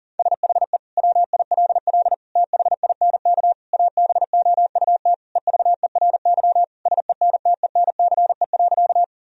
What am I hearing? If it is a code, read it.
Morse code, heard as a keyed tone, about 40 wpm, 700 Hz. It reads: SHE WILL THINK ABOUT EVERY SENTENCE.
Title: Sentences from Top 1000 Words - Review Part 1 - 40wpm